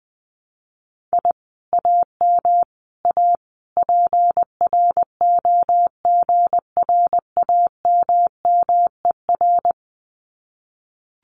Morse code, heard as a keyed tone, about 20 wpm, 700 Hz. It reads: I AM A PROGRAMMER
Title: Common QSO Phrases Encore 20wpm